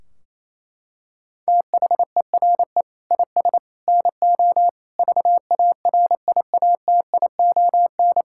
Morse code, heard as a keyed tone, about 28 wpm, 700 Hz. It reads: THERE IS NO VARIATION
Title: Sentences from Top 1600 Words 28wpm